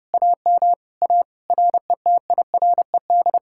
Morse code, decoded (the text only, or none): AM A RETIRED